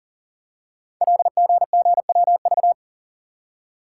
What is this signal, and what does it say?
Morse code, heard as a keyed tone, about 40 wpm, 700 Hz. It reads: LGGWV
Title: Code Groups - 5 Letters - 40wpm